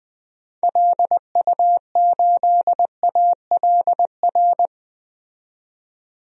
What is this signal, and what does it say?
Morse code, heard as a keyed tone, about 20 wpm, 700 Hz. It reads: LU8ALR
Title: DXCC-100 Call Signs - 6 Characters - 20wpm